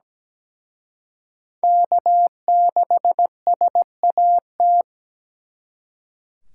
Morse code, heard as a keyed tone, about 20 wpm, 700 Hz. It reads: K6SAT
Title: Speed Racing - USA 1x3 Call Signs 17wpm